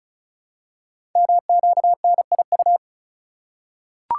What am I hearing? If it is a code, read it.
Morse code, heard as a keyed tone, about 36 wpm, 690 Hz. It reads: MQNIU
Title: Code Groups - 5 Letters - 35wpm